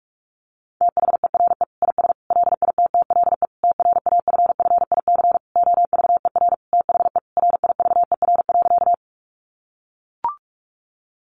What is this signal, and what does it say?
Morse code, heard as a keyed tone, about 44 wpm, 700 Hz. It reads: THERE IS LITTLE TRAFFIC OVER THE RIVER.
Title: Sentences from Top 1300 Words 45wpm